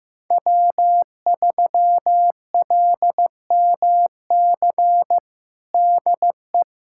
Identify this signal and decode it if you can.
Morse code, heard as a keyed tone, about 15 wpm, 700 Hz. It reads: W3LMC DE
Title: USA Callsign DE Callsign - 5 Characters - 15wpm